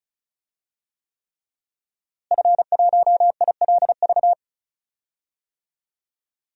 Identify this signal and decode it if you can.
Morse code, heard as a keyed tone, about 36 wpm, 700 Hz. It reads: F1ILV
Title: DXCC-100 Call Signs - 5 Characters - Repeated 3x - 35wpm